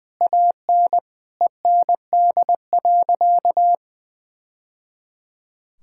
Morse code, heard as a keyed tone, about 20 wpm, 700 Hz. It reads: AN END.
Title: Sentences from Top 1000 Words - Review Part 5 - 20wpm